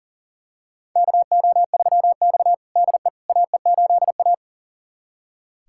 Morse code, heard as a keyed tone, about 40 wpm, 700 Hz. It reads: KO3X DE AE8A
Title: USA Callsign DE Callsign - 4 Characters - 40wpm